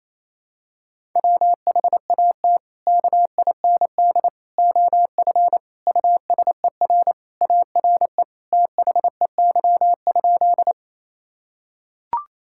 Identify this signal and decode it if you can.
Morse code, heard as a keyed tone, about 28 wpm, 700 Hz. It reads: WHAT KIND OF USER ARE THEY?
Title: Sentences from Top 1000 Words - Review Part 5 - 28wpm